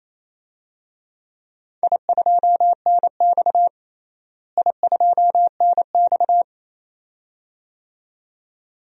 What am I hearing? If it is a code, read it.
Morse code, heard as a keyed tone, about 28 wpm, 700 Hz. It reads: I2NX I2NX
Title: DXCC-100 Call Signs - 4 Characters - Repeated 2x - 28wpm